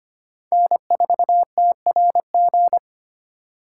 Morse code, heard as a keyed tone, about 32 wpm, 700 Hz. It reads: N4TRG